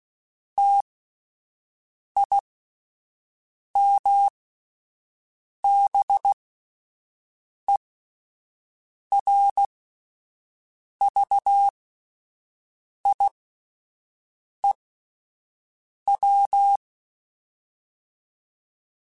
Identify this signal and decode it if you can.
Morse code, heard as a keyed tone, about 16 wpm, spaced as for 5 wpm, 800 Hz. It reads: TIMBERVIEW